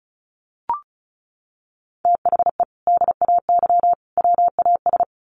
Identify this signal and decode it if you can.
Morse code, heard as a keyed tone, about 34 wpm, 700 Hz. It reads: THE DAY WAS
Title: Sentences from Top 1200 Words 35wpm